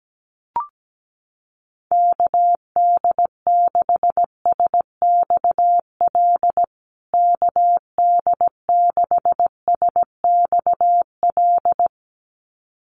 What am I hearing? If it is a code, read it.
Morse code, heard as a keyed tone, about 17 wpm, 700 Hz. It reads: KD6SXL KD6SXL
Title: USA Call Signs Repeated 2x 17wpm